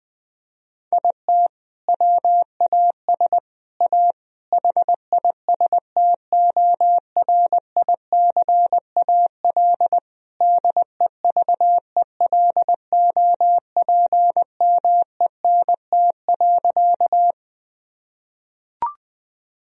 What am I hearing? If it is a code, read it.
Morse code, heard as a keyed tone, about 21 wpm, 700 Hz. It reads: IT WAS A HISTORICAL DEVELOPMENT.
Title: Sentences from Top 1000 Words - Review Part 4 - 20wpm